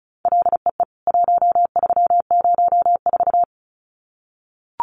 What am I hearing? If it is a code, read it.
Morse code, heard as a keyed tone, about 34 wpm, 700 Hz. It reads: LEE 1304
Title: CWT Exchange 35wpm